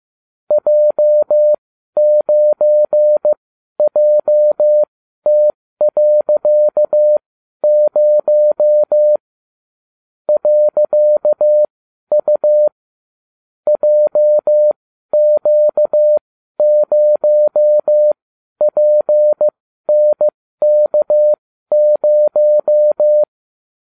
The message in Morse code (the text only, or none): J9JT.0 .U JQ0PNK0